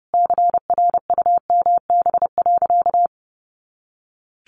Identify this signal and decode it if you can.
Morse code, heard as a keyed tone, about 28 wpm, 700 Hz. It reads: CRUMB.
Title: Additional Easy Phrases 30wpm